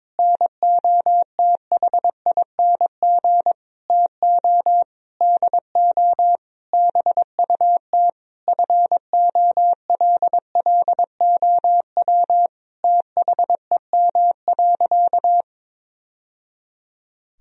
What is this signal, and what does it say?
Morse code, heard as a keyed tone, about 22 wpm, 700 Hz. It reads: NOTHING TO DO BUT FOLLOW THEM.
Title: Sentences from Top 1000 Words - Review Part 1 - 22wpm